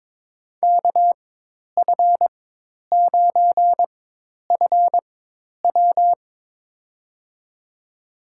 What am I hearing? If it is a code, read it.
Morse code, heard as a keyed tone, about 22 wpm, spaced as for 10 wpm, 700 Hz. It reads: KF9FW
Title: USA Call Signs - 5 Characters - 4x Character Spacing - 22wpm